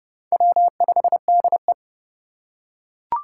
Morse code, heard as a keyed tone, about 30 wpm, 700 Hz. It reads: W5DE